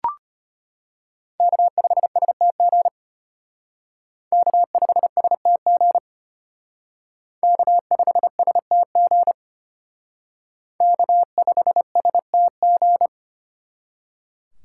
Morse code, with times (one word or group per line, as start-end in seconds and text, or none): 1.4-2.9 K5STG
4.3-6.0 K5STG
7.4-9.3 K5STG
10.8-13.1 K5STG